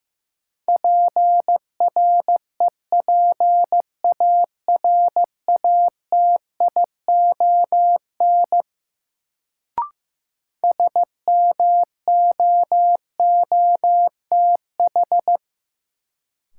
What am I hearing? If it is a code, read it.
Morse code, heard as a keyed tone, about 15 wpm, 700 Hz. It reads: PREPARATION SMOOTH
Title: Sentences from Top 1400 Words - New Word Drill - 15wpm